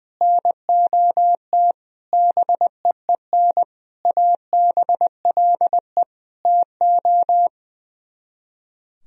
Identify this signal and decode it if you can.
Morse code, heard as a keyed tone, about 21 wpm, 700 Hz. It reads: NOT BEEN ABLE TO
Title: Speed-Racing - Sets of 5 Words 20wpm